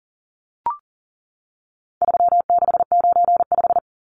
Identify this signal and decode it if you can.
Morse code, heard as a keyed tone, about 40 wpm, 700 Hz. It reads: 3695
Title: Numbers - 4-Digit 40wpm